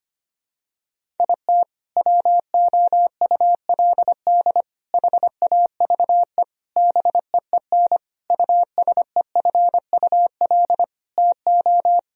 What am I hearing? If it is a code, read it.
Morse code, heard as a keyed tone, about 25 wpm, 700 Hz. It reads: IT WOULD HAVE BEEN USEFUL TO
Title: Sentences from Top 1000 Words - Review Part 4 - 25wpm